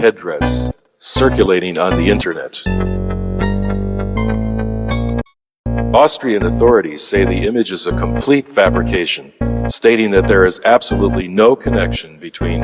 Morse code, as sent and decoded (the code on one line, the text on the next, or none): none